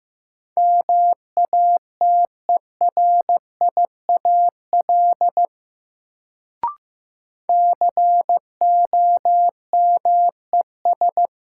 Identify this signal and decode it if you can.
Morse code, heard as a keyed tone, about 15 wpm, 700 Hz. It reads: MATERIAL COMES
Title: Sentences from Top 1000 Words - Review Part 5 - New Word Drill - 15wpm